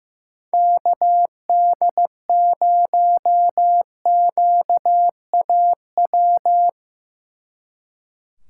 Morse code, heard as a keyed tone, about 16 wpm, 700 Hz. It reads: KD0QAW
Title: Speed Racing - USA Call Signs 15wpm